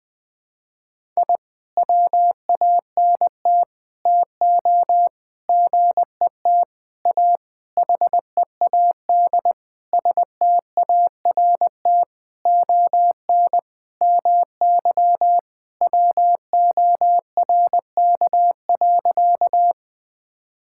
Morse code, heard as a keyed tone, about 21 wpm, 700 Hz. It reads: I WANT TO GET A HEAD START ON MY WORK.
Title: Sentences with 7-Word Phrases - Spoken Beginning - 20wpm